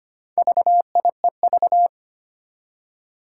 Morse code, heard as a keyed tone, about 25 wpm, 700 Hz. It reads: VIEV
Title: Four Challenging Letter-Number Combinations 25wpm